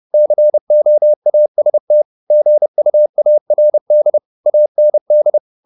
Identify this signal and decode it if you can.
Morse code, heard as a keyed tone, about 30 wpm, 600 Hz. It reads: COAST GUARD AND